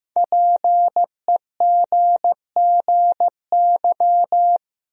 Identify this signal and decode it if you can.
Morse code, heard as a keyed tone, about 15 wpm, 700 Hz. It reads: PEGGY
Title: USA Common Names - 15wpm